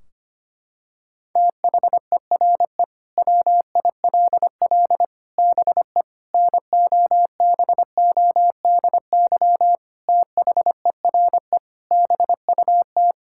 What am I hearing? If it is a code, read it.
Morse code, heard as a keyed tone, about 25 wpm, 700 Hz. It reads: THERE WILL BE NOBODY THERE BUT
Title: Sentences from Top 1600 Words 25wpm